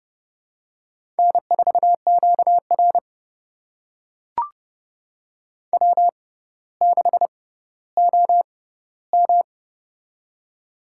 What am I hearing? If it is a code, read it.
Morse code, heard as a keyed tone, about 32 wpm, spaced as for 16 wpm, 700 Hz. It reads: N4QR W6OM
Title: USA Call Signs - 4 Characters - 6x Character Spacing - 30wpm